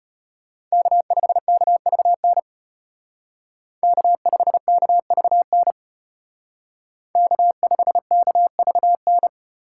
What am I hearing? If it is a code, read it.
Morse code, heard as a keyed tone, about 32 wpm, 700 Hz. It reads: K5KVN K5KVN K5KVN